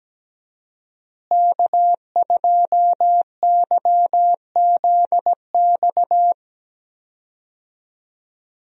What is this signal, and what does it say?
Morse code, heard as a keyed tone, about 17 wpm, 700 Hz. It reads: K2YZX
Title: USA Call Signs - 5 Characters - Repeated 3x - 17wpm